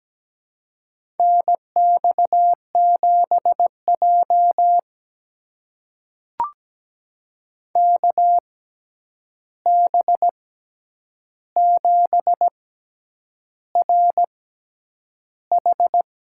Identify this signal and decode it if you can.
Morse code, heard as a keyed tone, about 17 wpm, spaced as for 8 wpm, 700 Hz. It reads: NX7J KB7RH
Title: USA Call Signs - 6x Character Spacing - 17wpm